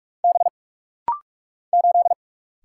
Morse code, heard as a keyed tone, about 46 wpm, 700 Hz. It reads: B 8
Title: Single Letter-Number - Rapid-Fire 45wpm